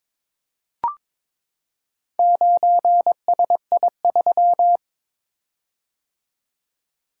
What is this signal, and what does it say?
Morse code, heard as a keyed tone, about 22 wpm, 700 Hz. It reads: 9SI3
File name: Four Letter-Number Combinations - 22wpm